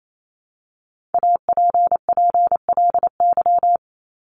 Morse code, heard as a keyed tone, about 28 wpm, 700 Hz. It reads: APPLY
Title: ED Suffix - Top 100 Words with Base Words 28wpm